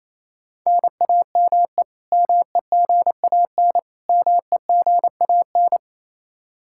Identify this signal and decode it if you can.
Morse code, heard as a keyed tone, about 28 wpm, 700 Hz. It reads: NAME MEGAN MEGAN
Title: Practice QSO - Exchange 1 - Beginning - 28wpm